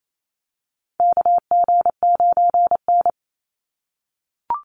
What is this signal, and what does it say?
Morse code, heard as a keyed tone, about 28 wpm, 700 Hz. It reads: KG9N
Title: Call Signs - 28wpm